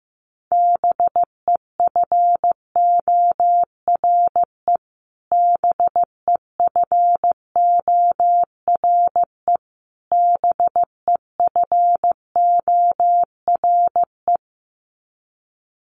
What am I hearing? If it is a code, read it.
Morse code, heard as a keyed tone, about 15 wpm, 700 Hz. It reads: BEFORE BEFORE BEFORE
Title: Top 200 Words Repeated 15wpm